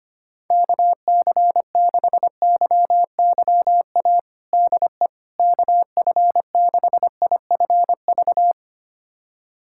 Morse code, heard as a keyed tone, about 25 wpm, 700 Hz. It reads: KC6YYA DE KF6IFV
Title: USA Callsign DE Callsign - 6 Characters - 25wpm